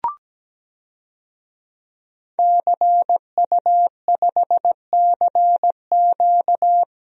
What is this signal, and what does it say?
Morse code, heard as a keyed tone, about 17 wpm, 700 Hz. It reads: CU5CQ
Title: DXCC-100 Call Signs - 5 Characters - Repeated 3x - 17wpm